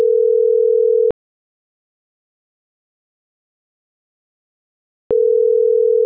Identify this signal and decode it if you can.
Ringback tone (North America), 440 + 480 Hz.